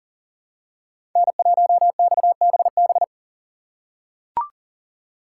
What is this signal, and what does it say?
Morse code, heard as a keyed tone, about 39 wpm, 700 Hz. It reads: N1XBB
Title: USA Call Signs - 4x Character Spacing - 40wpm